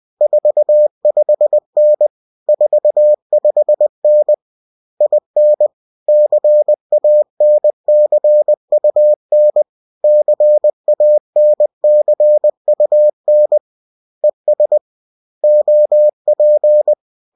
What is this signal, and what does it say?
Morse code, heard as a keyed tone, about 20 wpm, 600 Hz. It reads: 45N 45N IN CANCUN CANCUN ES OP